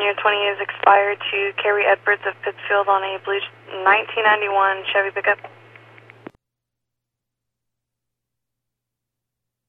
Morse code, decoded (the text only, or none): none